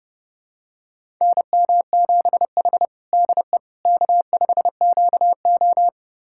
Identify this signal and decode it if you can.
Morse code, heard as a keyed tone, about 32 wpm, 700 Hz. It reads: NM7H DE K5QO
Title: USA Callsign DE Callsign - 4 Characters - 30wpm